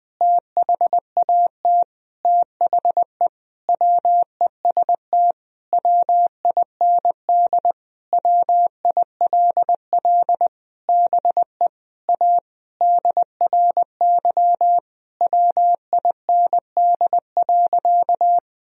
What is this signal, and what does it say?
Morse code, heard as a keyed tone, about 20 wpm, 700 Hz. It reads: THAT THE WEST WIND WILL BE A DRY WIND.
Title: Sentences from Top 1000 Words - Review Part 3 - 20wpm